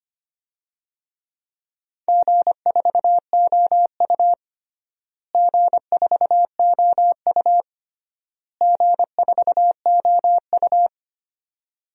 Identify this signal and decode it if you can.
Morse code, heard as a keyed tone, about 25 wpm, 700 Hz. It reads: G4OU G4OU G4OU
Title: DXCC-100 Call Signs - 4 Characters - Repeated 3x - 25wpm